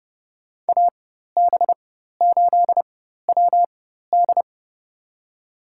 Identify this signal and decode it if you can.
Morse code, heard as a keyed tone, about 30 wpm, spaced as for 14 wpm, 700 Hz. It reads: AB8WD
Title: USA Call Signs - 4x Character Spacing - 30wpm